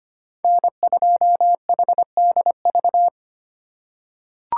Morse code, heard as a keyed tone, about 25 wpm, 700 Hz. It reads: N2HDV
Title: USA Call Signs - 5 Characters - 2x Character Spacing - 25wpm